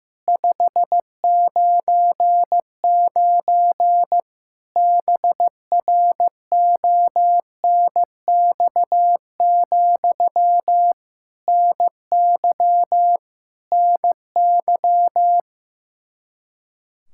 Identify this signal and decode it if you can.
Morse code, heard as a keyed tone, about 15 wpm, 700 Hz. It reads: 599 BRONX, NY NY
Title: Practice QSO - Exchange 1 - Beginning - 15wpm